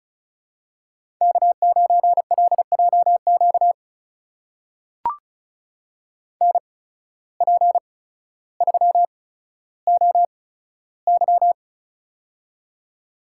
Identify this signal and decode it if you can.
Morse code, heard as a keyed tone, about 34 wpm, spaced as for 15 wpm, 700 Hz. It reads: K9LJQ NP3OY
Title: USA Call Signs - 5 Characters - 8x Character Spacing - 35wpm